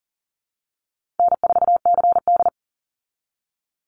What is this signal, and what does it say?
Morse code, heard as a keyed tone, about 40 wpm, 700 Hz. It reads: N4CD
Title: Easy Call Signs Repeated 3x - 40wpm